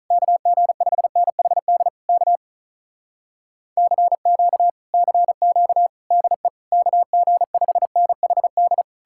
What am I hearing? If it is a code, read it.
Morse code, heard as a keyed tone, about 47 wpm, 700 Hz. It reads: KG5NHD K CQ CQ DE KG5NHD